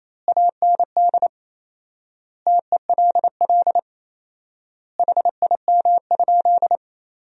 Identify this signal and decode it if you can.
Morse code, heard as a keyed tone, about 28 wpm, 700 Hz. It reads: AND TELL HIM?
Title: 4x Word-Spacing - Sentences from Top 200 Words - 28wpm